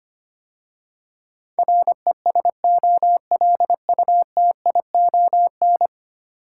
Morse code, heard as a keyed tone, about 25 wpm, 700 Hz. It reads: RESOLUTION
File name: 10 Letter Words - First 3 Letters Given - 25wpm